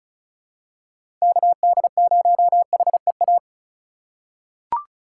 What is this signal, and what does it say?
Morse code, heard as a keyed tone, about 34 wpm, 700 Hz. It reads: KD0HEA
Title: USA Call Signs - 6 Characters - 3x Character Spacing - 35wpm